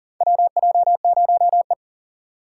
Morse code, heard as a keyed tone, about 37 wpm, 700 Hz. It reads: WJ0E
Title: USA Call Signs - 4 Characters - 4x Character Spacing - 40wpm